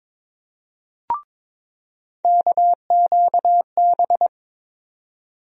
Morse code, heard as a keyed tone, about 22 wpm, 700 Hz. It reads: KQB
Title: USA Call Sign Postfixes - 3 Letters 22wpm